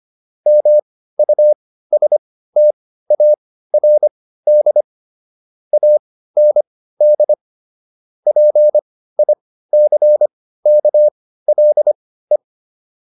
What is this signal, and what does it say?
Morse code, heard as a keyed tone, about 25 wpm, spaced as for 15 wpm, 600 Hz. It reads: MUSTARD AND PICKLE